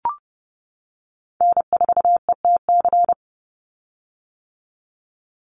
Morse code, heard as a keyed tone, about 30 wpm, 700 Hz. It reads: N4ETC
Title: Call Signs - 30wpm